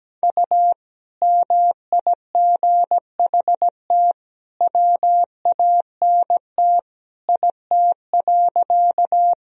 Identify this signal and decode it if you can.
Morse code, heard as a keyed tone, about 17 wpm, 700 Hz. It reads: U MIGHT WANT IT.
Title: Sentences from Top 300 Words and Abbreviations 17wpm